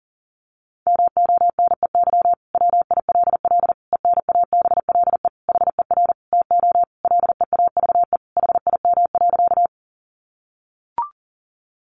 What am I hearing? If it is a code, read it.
Morse code, heard as a keyed tone, about 40 wpm, 700 Hz. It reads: MONEY WILL ENABLE HER TO LEAVE HIM.
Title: Sentences from Top 1500 Words 40wpm